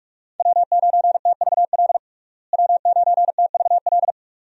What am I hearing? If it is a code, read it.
Morse code, heard as a keyed tone, about 46 wpm, 700 Hz. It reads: W9TVL W9TVL
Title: USA Call Signs - 5 Characters - Repeated 2x - 45wpm